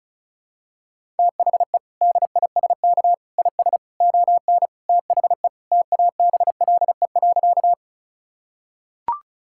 Morse code, heard as a keyed tone, about 36 wpm, 700 Hz. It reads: THE DISK IS ON THE TABLE.
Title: Sentences from Top 1000 Words - Review Part 6 - 35wpm